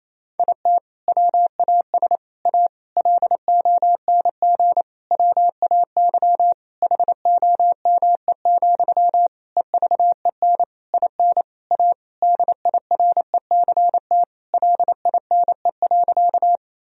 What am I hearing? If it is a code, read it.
Morse code, heard as a keyed tone, about 27 wpm, 700 Hz. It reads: IT WAS A LONG WAY HOME, EVEN IN A DIRECT LINE.
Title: Sentences from Top 1000 Words - Review Part 2 - 28wpm